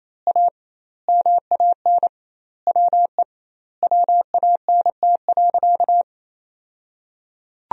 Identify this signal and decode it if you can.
Morse code, heard as a keyed tone, about 28 wpm, 700 Hz. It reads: A MAN WE WANT.
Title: 2x Word-Spacing - Sentences from Top 200 Words - 28wpm